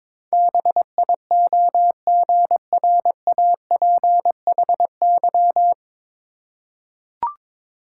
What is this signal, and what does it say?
Morse code, heard as a keyed tone, about 21 wpm, 690 Hz. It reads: BIOGRAPHY
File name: Difficult Words with Hints - 22wpm